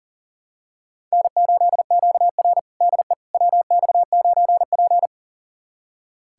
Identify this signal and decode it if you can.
Morse code, heard as a keyed tone, about 40 wpm, 690 Hz. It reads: N8QR DE WX9P
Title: USA Callsign DE Callsign - 4 Characters - 40wpm